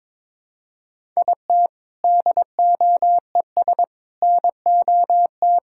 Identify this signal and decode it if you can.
Morse code, heard as a keyed tone, about 22 wpm, 700 Hz. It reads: IT DOES NOT
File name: practice / 22wpm / Sentences with 4-Word Phrases - 22wpm